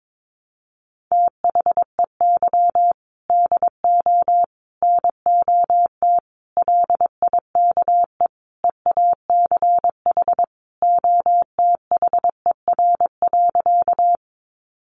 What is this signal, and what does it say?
Morse code, heard as a keyed tone, about 22 wpm, 700 Hz. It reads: THEY DO NOT LIKE EACH OTHER.
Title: Easy Phrases 22wpm